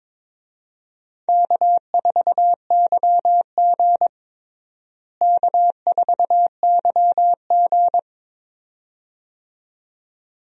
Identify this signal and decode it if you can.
Morse code, heard as a keyed tone, about 22 wpm, 700 Hz. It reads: K4YG K4YG
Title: USA Call Signs - 4 Characters - Repeated 2x - 22wpm